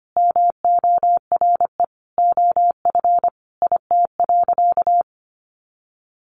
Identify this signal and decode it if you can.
Morse code, heard as a keyed tone, about 25 wpm, 700 Hz. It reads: MORE OF IT.
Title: Sentences from Top 1300 Words 25wpm